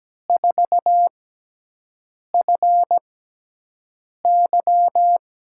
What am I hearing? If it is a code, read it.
Morse code, heard as a keyed tone, about 17 wpm, spaced as for 6 wpm, 700 Hz. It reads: 4FY
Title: USA Call Signs - 6 Characters - 6x Character Spacing - 17wpm